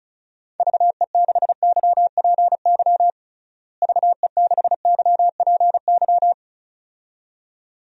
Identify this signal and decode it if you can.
Morse code, heard as a keyed tone, about 36 wpm, 700 Hz. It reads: VE6YPY VE6YPY
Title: DXCC-100 Call Signs - 6 Characters - Repeated 2x - 35wpm